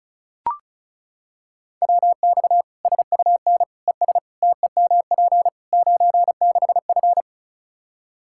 Morse code, heard as a keyed tone, about 34 wpm, 700 Hz. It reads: WX SUN ES TEMP 96F
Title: Practice QSO - Exchange 2 - Info - 35wpm